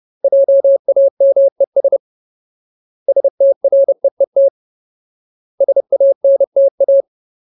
Morse code, heard as a keyed tone, about 30 wpm, 550 Hz. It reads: JAMES STREET SANTA
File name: Ultimate Word List / 30WPM Ultimate